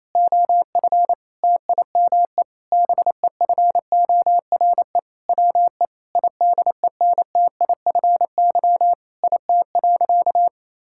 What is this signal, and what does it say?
Morse code, heard as a keyed tone, about 28 wpm, 700 Hz. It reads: OF TIME BEFORE WE IDENTIFY IT.
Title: Sentences from Top 1000 Words - Review Part 6 - 28wpm